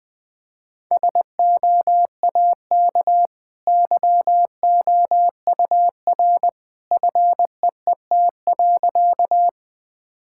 Morse code, heard as a keyed tone, about 20 wpm, 700 Hz. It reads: SOAK YOUR FEET.
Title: Easy Sentences with Hints - 20wpm